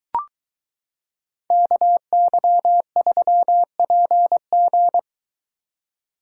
Morse code, heard as a keyed tone, about 23 wpm, 700 Hz. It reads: KY3PG